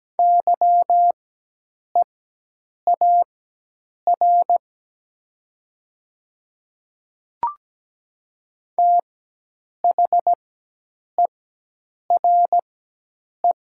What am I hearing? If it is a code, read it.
Morse code, heard as a keyed tone, about 17 wpm, spaced as for 8 wpm, 700 Hz. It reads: YEAR THERE
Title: 4x Character Spacing - Top 200 Words - 17wpm